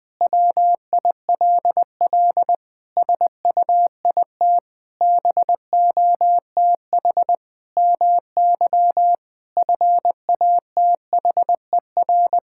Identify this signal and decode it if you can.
Morse code, heard as a keyed tone, about 20 wpm, 700 Hz. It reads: WILL SUIT BOTH MY FATHER
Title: Sentences from Top 1600 Words 20wpm